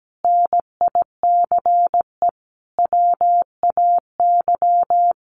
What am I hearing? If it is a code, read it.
Morse code, heard as a keyed tone, about 17 wpm, 700 Hz. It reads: NICE WAY
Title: Sentences from Top 1100 Words 17wpm